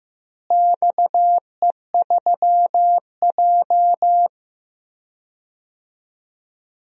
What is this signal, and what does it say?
Morse code, heard as a keyed tone, about 16 wpm, 700 Hz. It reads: XE3J
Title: DXCC-100 Call Signs - 4 Characters - Repeated 2x - 15wpm